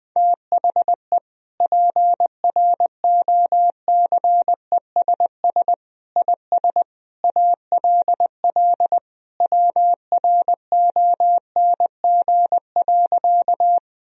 Morse code, heard as a keyed tone, about 21 wpm, 700 Hz. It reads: THE PROCESS IS ALL WRONG.
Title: Sentences from Top 1000 Words - Review Part 3 - 20wpm